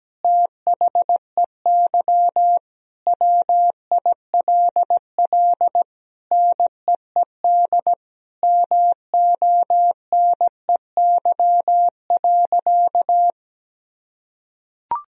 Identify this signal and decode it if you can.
Morse code, heard as a keyed tone, about 17 wpm, 700 Hz. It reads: THEY WILL NEED MONEY.
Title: Sentences from Top 1000 Words - Review Part 4 - 17wpm